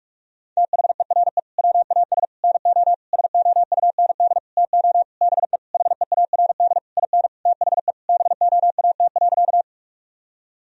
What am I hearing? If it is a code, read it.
Morse code, heard as a keyed tone, about 41 wpm, 700 Hz. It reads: THERE WAS NO SOUND TO BE HEARD IN THE BOAT.
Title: Sentences from Top 1000 Words - Review Part 3 - 45wpm